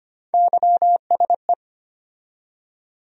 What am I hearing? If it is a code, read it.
Morse code, heard as a keyed tone, about 25 wpm, 700 Hz. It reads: YSE
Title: USA Call Sign Postfixes - 3 Letters 25wpm